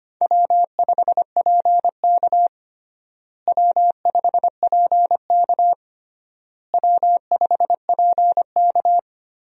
Morse code, heard as a keyed tone, about 25 wpm, 700 Hz. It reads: W5PK W5PK W5PK
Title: USA Call Signs - 4 Characters - Repeated 3x - 25wpm